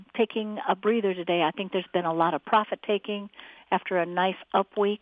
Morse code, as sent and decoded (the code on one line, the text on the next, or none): none